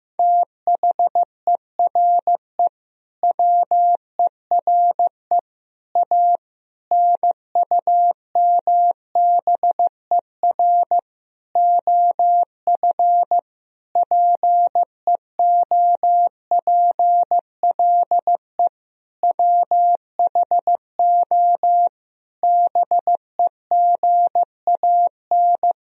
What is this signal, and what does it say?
Morse code, heard as a keyed tone, about 15 wpm, 700 Hz. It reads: THERE WERE A NUMBER OF PEOPLE WHO BEGAN
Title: Sentences from Top 1000 Words - Review Part 1 - 15wpm